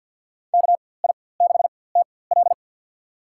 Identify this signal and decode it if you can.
Morse code, heard as a keyed tone, about 47 wpm, spaced as for 23 wpm, 700 Hz. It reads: KI6TL
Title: USA Call Signs - 5 Characters - 4x Character Spacing - 50wpm